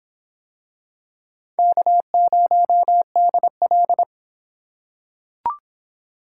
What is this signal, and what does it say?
Morse code, heard as a keyed tone, about 24 wpm, 700 Hz. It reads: K0DL